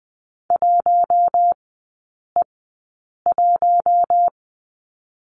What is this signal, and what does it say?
Morse code, heard as a keyed tone, about 20 wpm, spaced as for 8 wpm, 700 Hz. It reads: 1E1
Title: 053 Lesson 10e - Letter r - Three Characters 20wpm